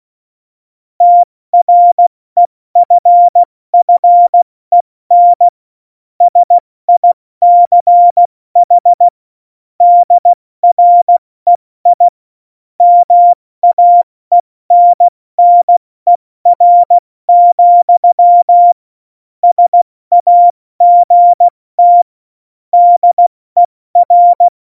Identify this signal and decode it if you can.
Morse code, heard as a keyed tone, about 16 wpm, 700 Hz. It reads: TREFFEN SICH DREI MAENNER, SAGT DER
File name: Morsekurs / Witze / Witz 01. September 2020, 70 F80